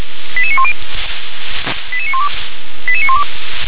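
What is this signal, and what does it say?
Five-tone selective call (ZVEI), heard as quick runs of five tones, tones of about 70 ms. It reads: 88910 88912 88912
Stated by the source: ZVEI five-tone call received by SDR.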